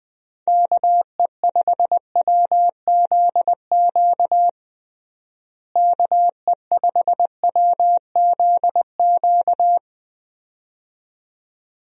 Morse code, heard as a keyed tone, about 20 wpm, 700 Hz. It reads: KE5WZQ KE5WZQ
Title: USA Call Signs - 6 Characters - Repeated 2x - 20wpm